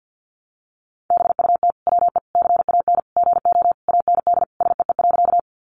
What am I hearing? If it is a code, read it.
Morse code, heard as a keyed tone, about 52 wpm, 700 Hz. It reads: BUT WE CAN GO AND SEE.
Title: Sentences from Top 100 Words 50wpm